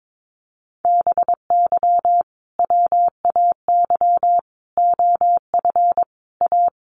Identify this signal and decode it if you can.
Morse code, heard as a keyed tone, about 22 wpm, 700 Hz. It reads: BY WAY OF A
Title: Sentences from Top 1500 Words 22wpm